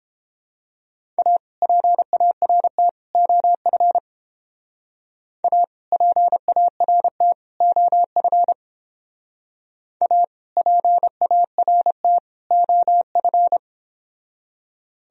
Morse code, heard as a keyed tone, about 29 wpm, 700 Hz. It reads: A PART OF A PART OF A PART OF